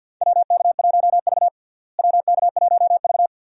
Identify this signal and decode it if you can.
Morse code, heard as a keyed tone, about 51 wpm, 700 Hz. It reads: WK1V WK1V
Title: USA Call Signs - 4 Characters - Repeated 2x - 50wpm